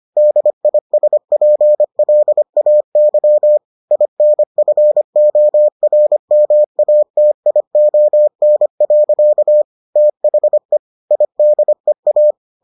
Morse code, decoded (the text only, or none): DISPLAY INFORMATION. THE IDEA